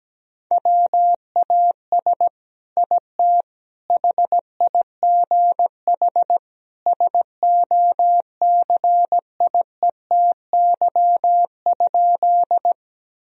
Morse code, heard as a keyed tone, about 17 wpm, 700 Hz. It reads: WAS IT HIGH SOCIETY?
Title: Sentences from Top 1000 Words - Review Part 4 - 17wpm